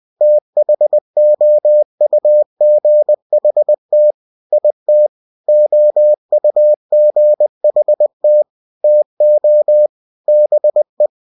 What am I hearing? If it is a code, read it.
Morse code, heard as a keyed tone, about 20 wpm, 600 Hz. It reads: THOUGHT IT OUGHT TO BE
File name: cw-books / winnie-the-pooh / winnie-20wpm-page-0087